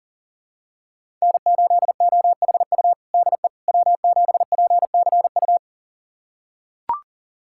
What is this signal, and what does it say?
Morse code, heard as a keyed tone, about 37 wpm, 700 Hz. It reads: N8OHU DE W7PCU
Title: USA Callsign DE Callsign - 5 Characters - 40wpm